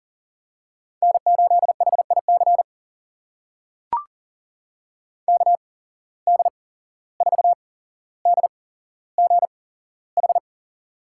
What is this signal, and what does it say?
Morse code, heard as a keyed tone, about 39 wpm, spaced as for 16 wpm, 700 Hz. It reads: N8HIC KD4DGH
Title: USA Call Signs - 8x Character Spacing - 40wpm